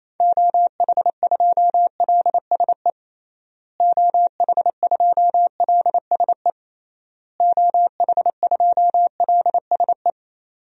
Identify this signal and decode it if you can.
Morse code, heard as a keyed tone, about 28 wpm, 700 Hz. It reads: OH2LSE OH2LSE OH2LSE
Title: DXCC-100 Call Signs - 6 Characters - Repeated 3x - 28wpm